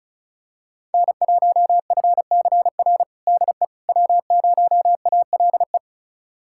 Morse code, heard as a keyed tone, about 34 wpm, 690 Hz. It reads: N1FCR DE W0ALE
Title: USA Callsign DE Callsign - 5 Characters - 35wpm